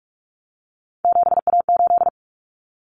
Morse code, heard as a keyed tone, about 44 wpm, 700 Hz. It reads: 7A8
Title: Common Cut Numbers - 3-Digit 45wpm